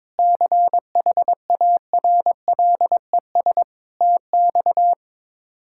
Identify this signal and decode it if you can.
Morse code, heard as a keyed tone, about 21 wpm, 700 Hz. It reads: CHARLES TX
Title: K1USN SST QSO Practice - 22wpm